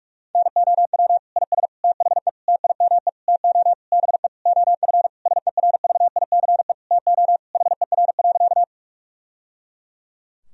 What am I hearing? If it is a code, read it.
Morse code, heard as a keyed tone, about 42 wpm, 700 Hz. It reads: NOW IS THE TIME TO BE OF SERVICE TO HER.
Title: Sentences from Top 1000 Words - Review Part 4 - 45wpm